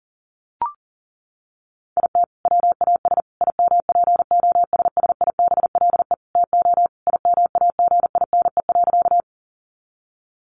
Morse code, heard as a keyed tone, about 40 wpm, 700 Hz. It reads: IT WAS IMPOSSIBLE TO IMAGINE.
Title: Sentences from Top 1400 Words 40wpm